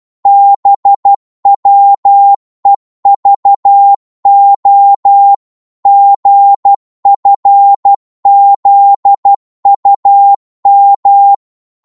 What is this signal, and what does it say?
Morse code, heard as a keyed tone, about 12 wpm, 800 Hz. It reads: BWEVO GFZUM
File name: Morse / ltr-12wpm